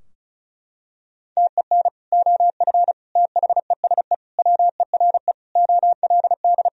Morse code, heard as a keyed tone, about 34 wpm, 700 Hz. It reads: TEN OF THESE WERE OLD
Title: Sentences from Top 1000 Words - Review Part 2 - 35wpm